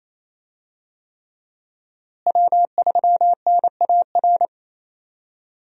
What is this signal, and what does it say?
Morse code, heard as a keyed tone, about 28 wpm, 700 Hz. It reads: W3NAR
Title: USA Call Signs - 5 Characters - Repeated 3x - 28wpm